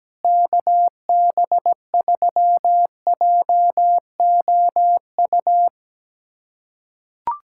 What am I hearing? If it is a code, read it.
Morse code, heard as a keyed tone, about 17 wpm, 700 Hz. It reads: KB3JOU
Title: USA Callsign DE Callsign - 6 Characters - 17wpm